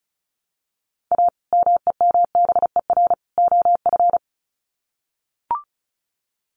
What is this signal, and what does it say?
Morse code, heard as a keyed tone, about 34 wpm, 700 Hz. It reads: A MEMBER OF
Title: Sets of 3 Words 35wpm